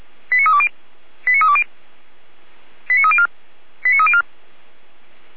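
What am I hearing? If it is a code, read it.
Five-tone selective call (ZVEI), heard as quick runs of five tones, tones of about 80 ms. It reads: 89320 89320 89304 89304
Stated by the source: ZVEI five-tone selective call 89250 received via SDR, fire brigade Orlamünde (feuerwehr-orlamuende).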